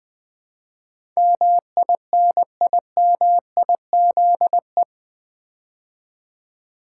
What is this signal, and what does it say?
Morse code, heard as a keyed tone, about 20 wpm, 700 Hz. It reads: MINIMIZE
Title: Build A Word - 4 Letters At A Time - 8-Letter Words - 20wpm